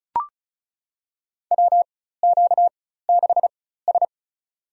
Morse code, heard as a keyed tone, about 36 wpm, spaced as for 16 wpm, 700 Hz. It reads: WQ6S
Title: USA Call Signs - 4x Character Spacing - 35wpm